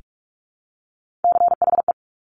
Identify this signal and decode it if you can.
Morse code, heard as a keyed tone, about 46 wpm, 700 Hz. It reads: CHE
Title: Top Three Letter Combinations in Words 45wpm